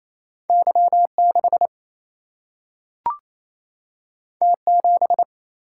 Morse code, heard as a keyed tone, about 28 wpm, 700 Hz. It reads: Y6 T7
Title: Single-Letter Single-Number 28wpm